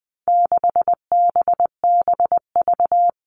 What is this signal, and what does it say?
Morse code, heard as a keyed tone, about 20 wpm, 700 Hz. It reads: 6BBV
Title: All Cut Numbers - 4-Digit 20wpm